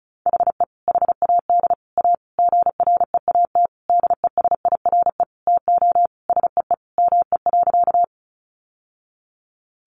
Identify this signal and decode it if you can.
Morse code, heard as a keyed tone, about 34 wpm, 700 Hz. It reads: HE HAD A GREAT DESIRE TO SEE ME.
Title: Sentences from Top 1100 Words 35wpm